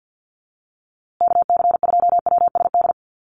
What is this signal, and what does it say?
Morse code, heard as a keyed tone, about 50 wpm, 700 Hz. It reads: KC2WSD
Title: USA Call Signs Repeated 3x 50wpm